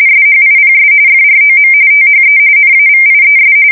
RTTY, about 45.45 baud. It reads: SHIPS FROM THE NAVIES
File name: fsk170 / USS-New-Jersey